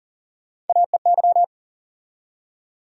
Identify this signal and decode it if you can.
Morse code, heard as a keyed tone, about 40 wpm, 700 Hz. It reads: AEY
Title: USA Call Sign Postfixes - 3 Letters 40wpm